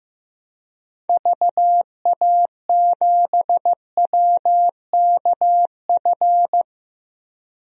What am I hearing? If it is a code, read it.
Morse code, heard as a keyed tone, about 15 wpm, 700 Hz. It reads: VA7WKF
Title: DXCC-100 Call Signs - 6 Characters - Repeated 3x - 15wpm